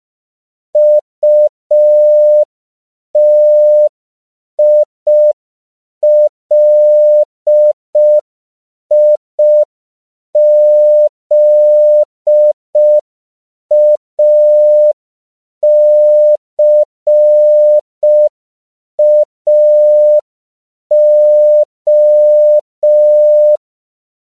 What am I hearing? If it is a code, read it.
Morse code, heard as a keyed tone, about 5 wpm, 600 Hz. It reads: UTILIZACAO
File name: CW / 5PPM